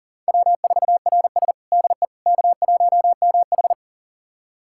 Morse code, heard as a keyed tone, about 40 wpm, 700 Hz. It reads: W4RS DE K1MH
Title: USA Callsign DE Callsign - 4 Characters - 40wpm